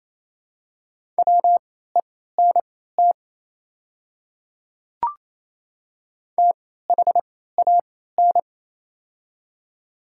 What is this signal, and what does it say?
Morse code, heard as a keyed tone, about 28 wpm, spaced as for 16 wpm, 700 Hz. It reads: WENT THAN
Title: 3x Character Spacing - Top 200 Words - 28wpm